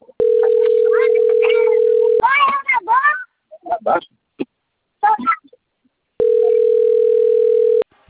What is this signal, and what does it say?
Ringback tone (North America), 440 + 480 Hz, with speech.
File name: out- - - - - .84538